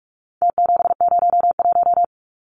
Morse code, heard as a keyed tone, about 44 wpm, 700 Hz. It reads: T701
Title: Numbers - Contest Style - 4-Digit 45wpm